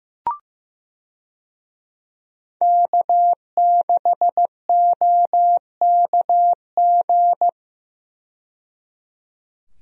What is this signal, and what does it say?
Morse code, heard as a keyed tone, about 15 wpm, 700 Hz. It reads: K6OKG
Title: USA Call Signs - 5 Characters - 15wpm